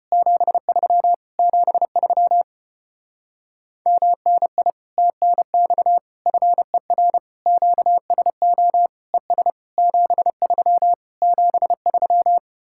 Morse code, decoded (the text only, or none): 73 73 MNI TNX FER QSO ES 73 73